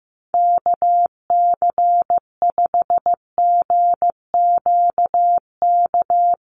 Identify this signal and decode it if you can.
Morse code, heard as a keyed tone, about 15 wpm, 700 Hz. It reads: KC5GQK
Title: USA Call Signs Repeated 2x 15wpm